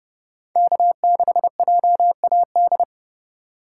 Morse code, heard as a keyed tone, about 27 wpm, 700 Hz. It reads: K6JAD